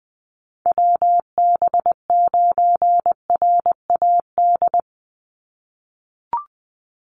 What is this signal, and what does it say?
Morse code, heard as a keyed tone, about 20 wpm, 700 Hz. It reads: WB9RAD
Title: USA Call Signs 20wpm